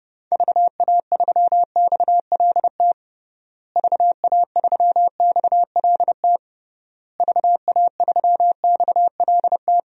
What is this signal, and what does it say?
Morse code, heard as a keyed tone, about 32 wpm, 700 Hz. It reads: VA3XLT VA3XLT VA3XLT
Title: DXCC-100 Call Signs - 6 Characters - Repeated 3x - 30wpm